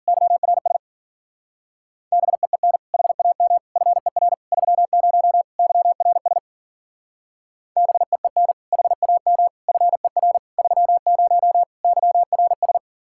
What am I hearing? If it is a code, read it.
Morse code, heard as a keyed tone, about 49 wpm, 700 Hz. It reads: YRS BEEN HAM FER 30 YRS BEEN HAM FER 30 YRS
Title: Speed-Racing Common QSO Phrases Encore - 40wpm